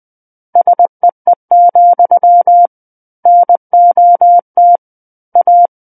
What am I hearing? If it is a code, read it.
Morse code, heard as a keyed tone, about 20 wpm, 700 Hz. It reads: SEE, NOT A
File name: 20 WPM